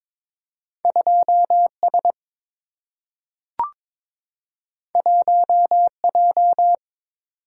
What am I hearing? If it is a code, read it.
Morse code, heard as a keyed tone, about 22 wpm, 700 Hz. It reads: 2 S 1 J
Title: Single-Letter Single-Number 22wpm